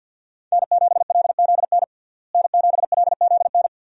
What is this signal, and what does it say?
Morse code, heard as a keyed tone, about 52 wpm, 700 Hz. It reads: N7LZN N7LZN
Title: USA Call Signs - 5 Characters - Repeated 2x - 50wpm